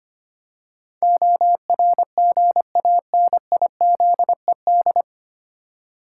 Morse code, heard as a keyed tone, about 25 wpm, 700 Hz. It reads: ORGANIZED
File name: Nine Letter Words - 25wpm